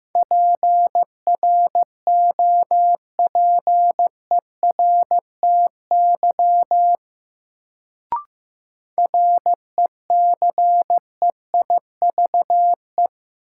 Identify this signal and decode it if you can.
Morse code, heard as a keyed tone, about 15 wpm, 700 Hz. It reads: PROPERTY RECEIVE
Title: Sentences from Top 1000 Words - New Word Drill - 15wpm